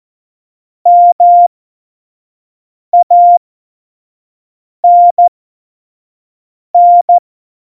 Morse code, heard as a keyed tone, about 14 wpm, spaced as for 5 wpm, 700 Hz. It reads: MANN
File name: Morsekurs / Witze / Witz 26 Mai 2020, 25 F70